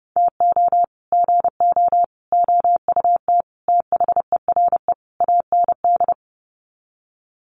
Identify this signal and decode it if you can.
Morse code, heard as a keyed tone, about 30 wpm, 700 Hz. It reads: TO GO OUT THERE AND
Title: Sets of 5 Words Encore2 30wpm